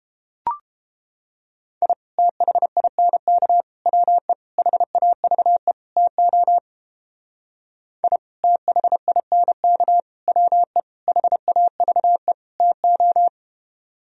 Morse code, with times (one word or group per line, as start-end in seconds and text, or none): 1.8-1.9 I
2.2-3.6 THINK
3.9-4.3 WE
4.6-5.7 HAVE
6.0-6.6 TO
8.0-8.2 I
8.4-10.0 THINK
10.3-10.8 WE
11.1-12.3 HAVE
12.6-13.3 TO